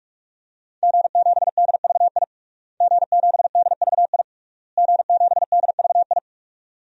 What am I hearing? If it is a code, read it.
Morse code, heard as a keyed tone, about 41 wpm, 700 Hz. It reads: G7DVI G7DVI G7DVI
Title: DXCC-100 Call Signs - 5 Characters - Repeated 3x - 45wpm